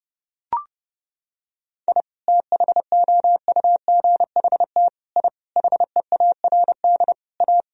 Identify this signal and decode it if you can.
Morse code, heard as a keyed tone, about 30 wpm, 700 Hz. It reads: I THOUGHT I HEARD A
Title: Sentences from Top 1000 Words - Review Part 1 - 30wpm